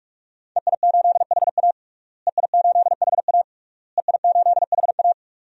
Morse code, heard as a keyed tone, about 42 wpm, 700 Hz. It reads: EI8HA EI8HA EI8HA
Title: DXCC-100 Call Signs - 5 Characters - Repeated 3x - 45wpm